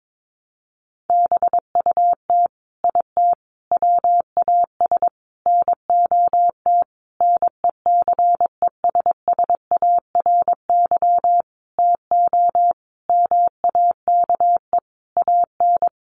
Morse code, heard as a keyed tone, about 22 wpm, 700 Hz. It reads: BUT IT WAS NOT NECESSARY TO MAKE AN
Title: Sentences from Top 1200 Words 22wpm